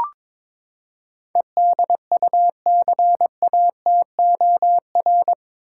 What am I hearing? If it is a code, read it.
Morse code, heard as a keyed tone, about 22 wpm, 700 Hz. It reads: EDUCATOR
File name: practice / 22wpm / Eight Letter Words - 22wpm